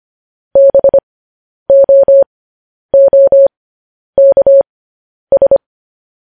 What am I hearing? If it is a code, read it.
Morse code, heard as a keyed tone, about 25 wpm, spaced as for 10 wpm, 550 Hz. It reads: BOOKS
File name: podcast / News headlines-1x-25x10